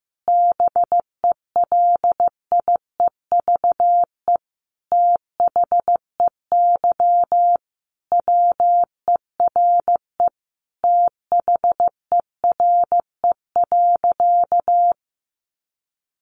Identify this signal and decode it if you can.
Morse code, heard as a keyed tone, about 14 wpm, 700 Hz. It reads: BELIEVE THEY WERE THERE.
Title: Sentences from Top 1000 Words 15wpm